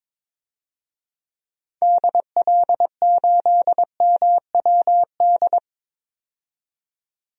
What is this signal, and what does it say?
Morse code, heard as a keyed tone, about 21 wpm, 700 Hz. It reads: DL8MWD
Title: DXCC-100 Call Signs - 6 Characters - 22wpm